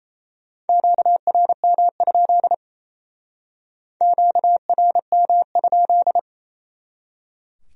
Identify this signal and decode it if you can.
Morse code, heard as a keyed tone, about 36 wpm, 700 Hz. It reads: QRM? QRM?